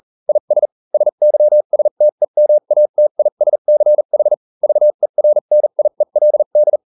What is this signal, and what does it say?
Morse code, heard as a keyed tone, about 40 wpm, spaced as for 35 wpm, 600 Hz. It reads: IS SYSTEMATISCH VERNIELD